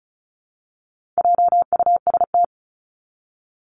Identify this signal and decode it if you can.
Morse code, heard as a keyed tone, about 34 wpm, 700 Hz. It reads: JUST
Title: Top 300 Words Repeated 35wpm